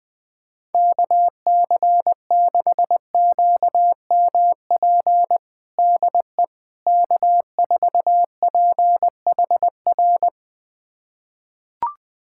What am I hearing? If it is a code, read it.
Morse code, heard as a keyed tone, about 21 wpm, 700 Hz. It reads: KC6QMP DE K4PHR
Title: USA Callsign DE Callsign 20wpm